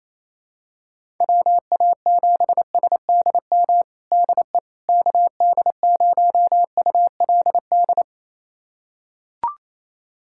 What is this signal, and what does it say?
Morse code, heard as a keyed tone, about 28 wpm, 700 Hz. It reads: WA7SDM DE KD0ULD
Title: USA Callsign DE Callsign - 6 Characters - 28wpm